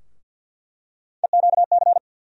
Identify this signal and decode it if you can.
Morse code, heard as a keyed tone, about 53 wpm, 700 Hz. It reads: EQC
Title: USA Call Sign Postfixes - 3 Letters 50wpm